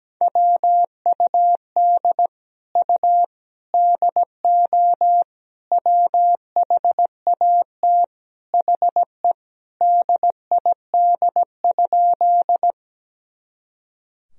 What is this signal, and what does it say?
Morse code, heard as a keyed tone, about 17 wpm, 700 Hz. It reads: WUD U DO WHAT HE DID?
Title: Sentences from Top 100 Words and Abbreviations 17wpm